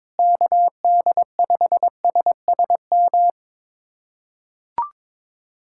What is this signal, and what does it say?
Morse code, heard as a keyed tone, about 22 wpm, 700 Hz. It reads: KD5SSM
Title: USA Callsign DE Callsign - 6 Characters - 22wpm